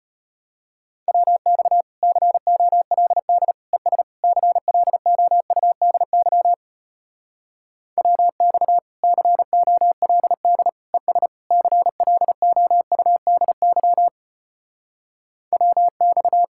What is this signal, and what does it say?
Morse code, heard as a keyed tone, about 33 wpm, 700 Hz. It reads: WX COLD ES CLOUDY WX COLD ES CLOUDY WX